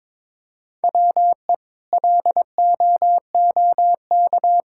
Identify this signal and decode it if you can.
Morse code, heard as a keyed tone, about 22 wpm, 700 Hz. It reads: WE LOOK
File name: practice / 22wpm / Sets of 6 Words - 22wpm